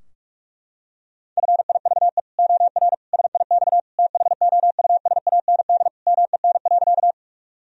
Morse code, heard as a keyed tone, about 41 wpm, 700 Hz. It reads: FIVE OR SIX THOUSAND MEN.
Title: Sentences from Top 1000 Words - Review Part 3 - 45wpm